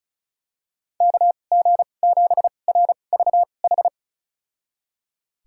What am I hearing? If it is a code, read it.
Morse code, heard as a keyed tone, about 36 wpm, spaced as for 25 wpm, 700 Hz. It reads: KG7RVH
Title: USA Call Signs - 6 Characters - 2x Character Spacing - 35wpm